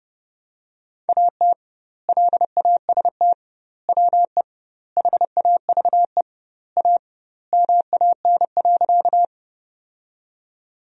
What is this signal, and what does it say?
Morse code, heard as a keyed tone, about 30 wpm, 700 Hz. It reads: AT LAST WE HAVE A MAN.
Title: 2x Word-Spacing - Sentences from Top 300 Words - 30wpm